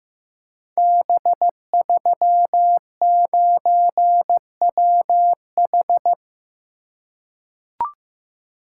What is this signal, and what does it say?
Morse code, heard as a keyed tone, about 15 wpm, 700 Hz. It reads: B39WH
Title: Code Groups - 5 Letter:Numbers - 15wpm